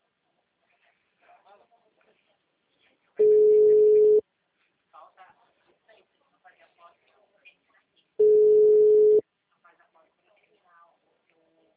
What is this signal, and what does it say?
Ringback tone (Europe, 425 Hz), with speech.